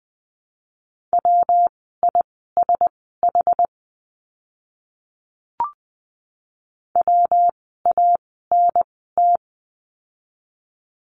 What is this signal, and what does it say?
Morse code, heard as a keyed tone, about 20 wpm, spaced as for 15 wpm, 700 Hz. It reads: WISH WANT
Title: 115 Lesson 19f - Letter w - Words 20wpm